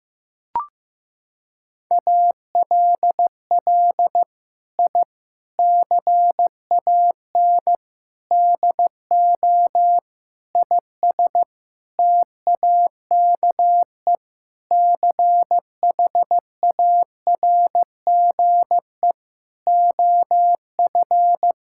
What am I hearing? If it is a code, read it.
Morse code, heard as a keyed tone, about 15 wpm, 700 Hz. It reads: ALL I CAN DO IS TAKE CHARGE OF
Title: Sentences from Top 1000 Words - Review Part 6 - 15wpm